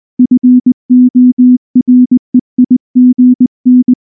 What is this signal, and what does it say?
Morse code, heard as a keyed tone, about 19 wpm, 250 Hz. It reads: FOREIGN
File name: morse / morse-training-20WPM-2